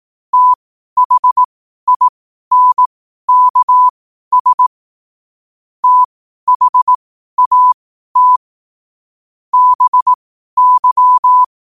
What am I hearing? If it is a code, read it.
Morse code, heard as a keyed tone, about 18 wpm, spaced as for 12 wpm, 1000 Hz. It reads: THINKS THAT BY
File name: qotdfiles / 12wpm